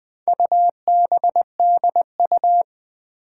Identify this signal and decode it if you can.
Morse code, heard as a keyed tone, about 20 wpm, 700 Hz. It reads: UBDU
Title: Four Challenging Letter-Number Combinations 20wpm